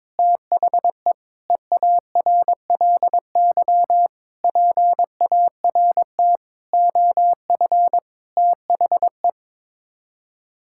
Morse code, heard as a keyed tone, about 22 wpm, 700 Hz. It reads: THE EARLY PART OF THE
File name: practice / 22wpm / Sets of 6 Words - 22wpm